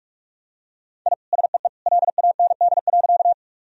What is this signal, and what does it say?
Morse code, heard as a keyed tone, about 46 wpm, 700 Hz. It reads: I SEE LAND.
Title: Sentences from Top 1000 Words - Review Part 1 - 45wpm